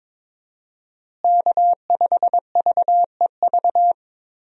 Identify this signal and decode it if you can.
Morse code, heard as a keyed tone, about 22 wpm, 700 Hz. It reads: K5VEV
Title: USA Call Signs - 5 Characters - Repeated 3x - 22wpm